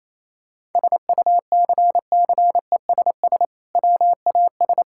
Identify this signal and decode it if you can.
Morse code, heard as a keyed tone, about 28 wpm, 700 Hz. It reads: SUCCESS WAS
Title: Sentences from Top 1000 Words - Review Part 6 - 28wpm